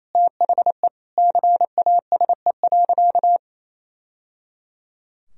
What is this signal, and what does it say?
Morse code, heard as a keyed tone, about 28 wpm, 700 Hz. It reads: THE CASE.
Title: Sentences from Top 1000 Words - Review Part 3 - 28wpm